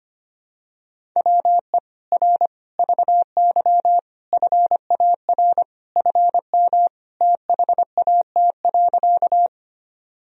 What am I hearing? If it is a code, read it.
Morse code, heard as a keyed tone, about 25 wpm, 700 Hz. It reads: WE R VY FAR FM THAT.
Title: Sentences from Top 300 Words and Abbreviations 25wpm